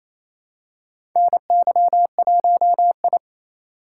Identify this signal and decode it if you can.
Morse code, heard as a keyed tone, about 28 wpm, 700 Hz. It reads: NY1I
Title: USA Call Signs - 4 Characters - 3x Character Spacing - 28wpm